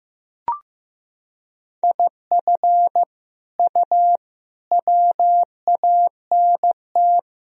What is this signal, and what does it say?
Morse code, heard as a keyed tone, about 15 wpm, 700 Hz. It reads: IF U WANT
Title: Sentences from Top 200 Words and Abbreviations 15wpm